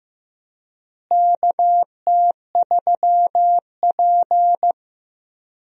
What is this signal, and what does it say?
Morse code, heard as a keyed tone, about 15 wpm, 700 Hz. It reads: KT3P
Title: USA Call Signs - 4 Characters - Repeated 2x - 15wpm